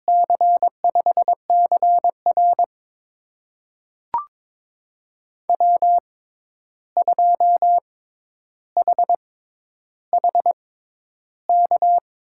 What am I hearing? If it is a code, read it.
Morse code, heard as a keyed tone, about 22 wpm, spaced as for 11 wpm, 700 Hz. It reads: C5CR W2HHK
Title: USA Call Signs - 5 Characters - 6x Character Spacing - 22wpm